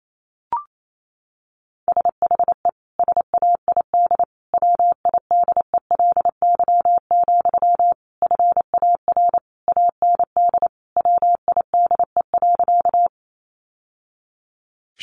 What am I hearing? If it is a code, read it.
Morse code, heard as a keyed tone, about 28 wpm, 700 Hz. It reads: SHE SAID WIDELY, FAR AND WIDE.
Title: Sentences from Top 1200 Words 28wpm